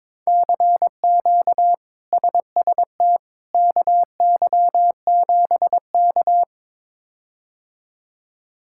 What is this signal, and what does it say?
Morse code, heard as a keyed tone, about 21 wpm, 700 Hz. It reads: CQ SST KY7K
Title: K1USN SST QSO Practice - 22wpm